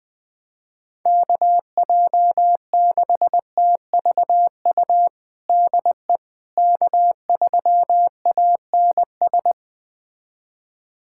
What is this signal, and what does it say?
Morse code, heard as a keyed tone, about 20 wpm, 700 Hz. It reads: KJ6TVU DE K3ANS
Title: USA Callsign DE Callsign 20wpm